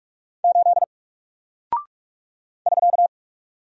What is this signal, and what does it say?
Morse code, heard as a keyed tone, about 44 wpm, 700 Hz. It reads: 8 <SK>
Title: Single Character (Letter:Number:Punctuation:Prosign) 45wpm